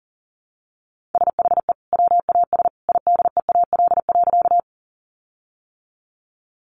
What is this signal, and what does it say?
Morse code, heard as a keyed tone, about 40 wpm, 700 Hz. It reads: SHE WAS IDEAL.
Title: Sentences from Top 1400 Words 40wpm